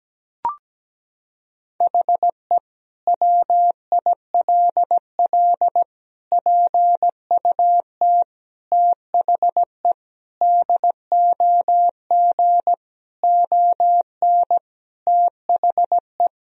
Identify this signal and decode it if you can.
Morse code, heard as a keyed tone, about 17 wpm, 700 Hz. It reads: HE WILL PUT THE DOG ON THE
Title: Sentences from Top 1000 Words - Review Part 2 - 17wpm